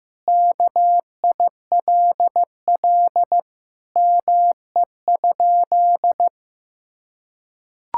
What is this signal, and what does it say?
Morse code, heard as a keyed tone, about 15 wpm, 700 Hz. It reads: KILL ME?
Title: Sentences from Top 1600 Words 15wpm